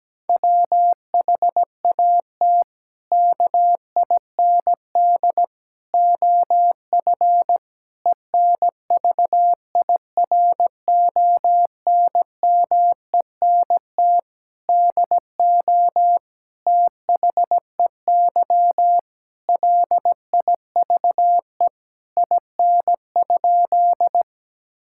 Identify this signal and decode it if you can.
Morse code, heard as a keyed tone, about 17 wpm, 700 Hz. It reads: WHAT KIND OF ENVIRONMENT DO THEY LIVE IN?
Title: Sentences from Top 1000 Words - Review Part 5 - 17wpm